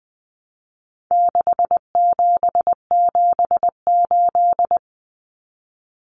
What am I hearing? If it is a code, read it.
Morse code, heard as a keyed tone, about 20 wpm, 700 Hz. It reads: 6778
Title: Numbers - 4-Digit 20wpm